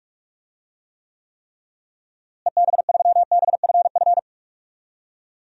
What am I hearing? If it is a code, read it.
Morse code, heard as a keyed tone, about 42 wpm, 700 Hz. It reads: EB3BFF
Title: DXCC-100 Call Signs - 6 Characters - Repeated 3x - 45wpm